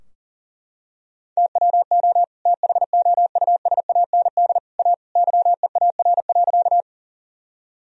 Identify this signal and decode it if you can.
Morse code, heard as a keyed tone, about 40 wpm, 700 Hz. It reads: TWO THOUSAND A YEAR.
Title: Sentences from Top 1000 Words - Review Part 3 - 40wpm